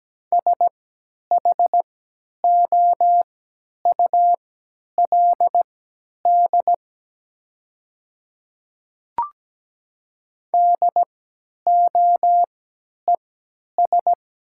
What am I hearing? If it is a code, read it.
Morse code, heard as a keyed tone, about 17 wpm, spaced as for 10 wpm, 700 Hz. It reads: SHOULD DOES
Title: 3x Character Spacing - Top 300 Words - 17wpm